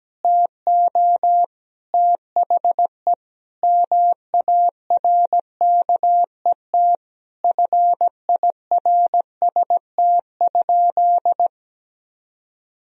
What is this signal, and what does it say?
Morse code, heard as a keyed tone, about 17 wpm, 700 Hz. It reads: TO THE MARKET FIRST?
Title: Sentences from Top 1000 Words - Review Part 3 - 17wpm